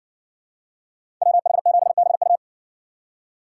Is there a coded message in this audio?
Yes: Morse code from 1.2 to 2.4 s, heard as a keyed tone, about 56 wpm, 700 Hz. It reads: WH7BU